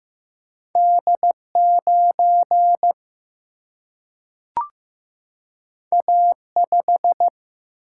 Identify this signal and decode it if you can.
Morse code, heard as a keyed tone, about 15 wpm, 700 Hz. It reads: D 9 A 5
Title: Single-Letter Single-Number 15wpm